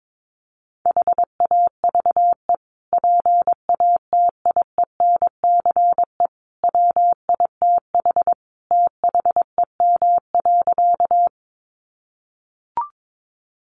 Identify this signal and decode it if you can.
Morse code, heard as a keyed tone, about 22 wpm, 700 Hz. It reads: HAVE PATIENCE WITH THEM.
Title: Sentences from Top 1300 Words 22wpm